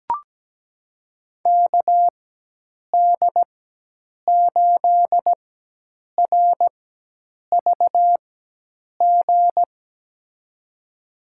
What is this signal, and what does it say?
Morse code, heard as a keyed tone, about 17 wpm, spaced as for 8 wpm, 700 Hz. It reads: KD8RVG
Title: USA Call Signs - 4x Character Spacing - 17wpm